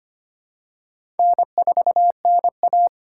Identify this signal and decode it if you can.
Morse code, heard as a keyed tone, about 25 wpm, 700 Hz. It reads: N4NA
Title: USA Call Signs - 4 Characters - 4x Character Spacing - 25wpm